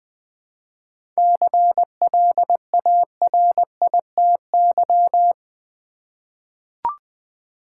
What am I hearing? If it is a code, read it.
Morse code, heard as a keyed tone, about 20 wpm, 700 Hz. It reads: CLARITY
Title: 7 Letter Words - First 3 Letters Given - 20wpm